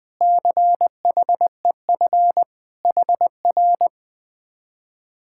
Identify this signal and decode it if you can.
Morse code, heard as a keyed tone, about 20 wpm, 700 Hz. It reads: CHEF HR
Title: Speed-Racing Common QSO Phrases Encore - 17wpm